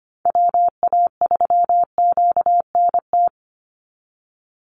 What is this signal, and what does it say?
Morse code, heard as a keyed tone, about 25 wpm, 700 Hz. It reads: WA3QNT
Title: Call Signs Repeated 2x - 25wpm